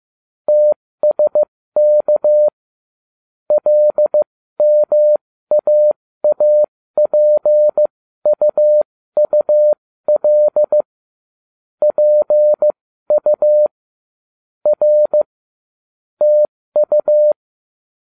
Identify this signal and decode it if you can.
Morse code, heard as a keyed tone, about 15 wpm, spaced as for 13 wpm, 600 Hz. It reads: TSK LMAAPUUL PU R TU
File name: cw / ubacw08 2